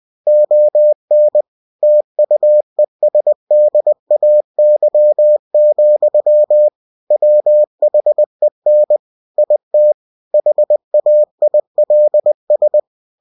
Morse code, heard as a keyed tone, about 20 wpm, 600 Hz. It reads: ON TUESDAY, WHEN IT HAILS
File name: cw-books / winnie-the-pooh / winnie-20wpm-page-0068